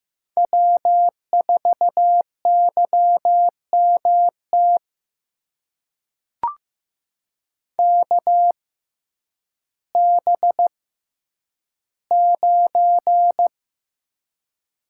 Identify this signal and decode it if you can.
Morse code, heard as a keyed tone, about 15 wpm, spaced as for 9 wpm, 700 Hz. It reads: W4YMT KB9
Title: USA Call Signs - 5 Characters - 6x Character Spacing - 15wpm